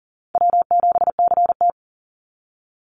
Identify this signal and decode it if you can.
Morse code, heard as a keyed tone, about 41 wpm, 700 Hz. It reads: W7CT
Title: Call Signs - 40wpm